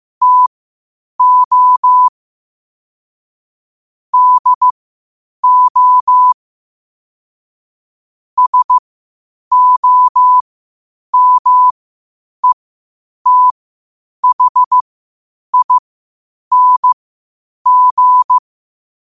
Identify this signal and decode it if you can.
Morse code, heard as a keyed tone, about 15 wpm, spaced as for 8 wpm, 1000 Hz. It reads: TO DO SOMETHING
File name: qotdfiles / 8wpm